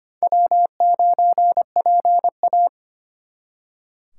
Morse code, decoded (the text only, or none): W9PA